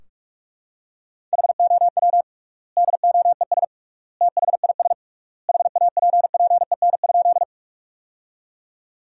Morse code, heard as a keyed tone, about 44 wpm, 700 Hz. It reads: HOW DOES THIS HAPPEN?
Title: 3x Word-Spacing - Sentences from Top 400 Words - 45wpm